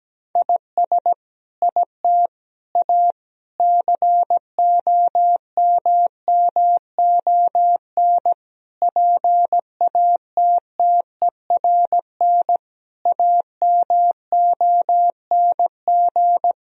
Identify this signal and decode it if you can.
Morse code, heard as a keyed tone, about 17 wpm, 700 Hz. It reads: IS IT A COMMON PATTERN AMONG
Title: Sentences from Top 1000 Words - Review Part 2 - 17wpm